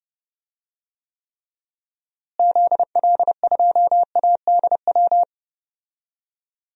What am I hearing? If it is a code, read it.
Morse code, heard as a keyed tone, about 32 wpm, 700 Hz. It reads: ZL2ADW
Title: DXCC-100 Call Signs - 6 Characters - Repeated 3x - 30wpm